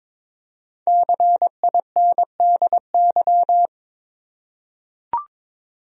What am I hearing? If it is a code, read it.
Morse code, heard as a keyed tone, about 22 wpm, 700 Hz. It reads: CINDY
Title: USA Common Names - Max 5 Letters - 22wpm